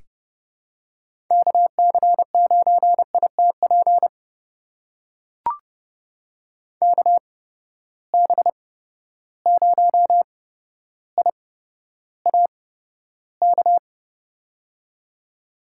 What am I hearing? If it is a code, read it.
Morse code, heard as a keyed tone, about 30 wpm, spaced as for 13 wpm, 700 Hz. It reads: KC9ITP KB0IAK
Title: USA Call Signs - 6 Characters - 8x Character Spacing - 30wpm